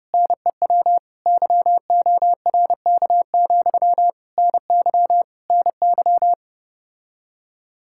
Morse code, heard as a keyed tone, about 30 wpm, 700 Hz. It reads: NEW YORK, NY NY